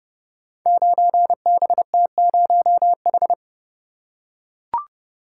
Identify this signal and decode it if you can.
Morse code, heard as a keyed tone, about 30 wpm, 700 Hz. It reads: 9BT0H
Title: Code Groups - 5 Letter:Numbers - 30wpm